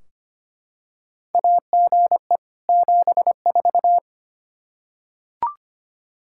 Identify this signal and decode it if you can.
Morse code, heard as a keyed tone, about 25 wpm, 700 Hz. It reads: AGE 74
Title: K1USN SST QSO Practice 25wpm